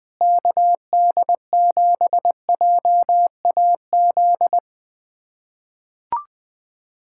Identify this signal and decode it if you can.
Morse code, heard as a keyed tone, about 20 wpm, 700 Hz. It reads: KD7JAZ
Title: USA Callsign DE Callsign - 6 Characters - 20wpm